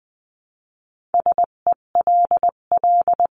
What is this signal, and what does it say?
Morse code, heard as a keyed tone, about 20 wpm, 700 Hz. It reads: SELL
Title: 258 Lesson 42c - Words 16-20wpm